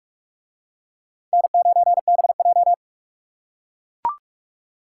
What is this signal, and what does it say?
Morse code, heard as a keyed tone, about 41 wpm, 700 Hz. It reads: N9BJ